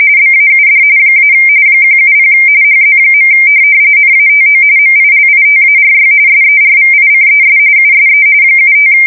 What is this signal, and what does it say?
RTTY, about 45.45 baud. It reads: BAUD - 170 HZ. NATO MILITARY SERVICES USE 75
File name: fsk170 / teletype-RTTY